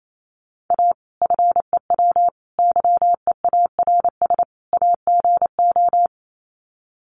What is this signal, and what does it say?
Morse code, heard as a keyed tone, about 28 wpm, 700 Hz. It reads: A FEW YEARS AGO
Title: Sets of 4 Words 28wpm